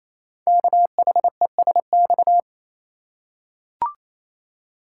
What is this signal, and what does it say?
Morse code, heard as a keyed tone, about 29 wpm, 700 Hz. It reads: KHESX
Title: Code Groups - 5 Letters - 28wpm